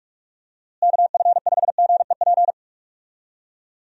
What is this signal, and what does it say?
Morse code, heard as a keyed tone, about 46 wpm, 700 Hz. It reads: KF5GEP
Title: USA Call Signs - 6 Characters - Repeated 2x - 45wpm